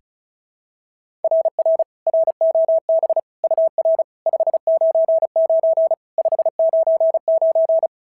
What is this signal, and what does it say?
Morse code, heard as a keyed tone, about 36 wpm, 660 Hz. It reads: RR ROB UR 599 599
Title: SOTA QSO Practice 35wpm